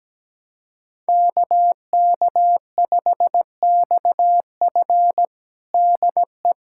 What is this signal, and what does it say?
Morse code, heard as a keyed tone, about 17 wpm, 700 Hz. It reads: KK5XF DE
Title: USA Callsign DE Callsign 17wpm